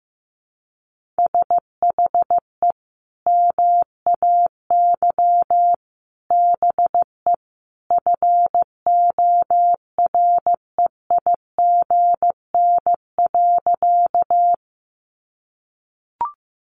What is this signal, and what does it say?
Morse code, heard as a keyed tone, about 14 wpm, 700 Hz. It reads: SHE MAY BE FOREIGN.
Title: Sentences from Top 1400 Words 15wpm